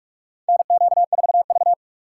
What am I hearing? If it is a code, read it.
Morse code, heard as a keyed tone, about 45 wpm, 700 Hz. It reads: NQ4V
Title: USA Call Signs - 3x Character Spacing - 45wpm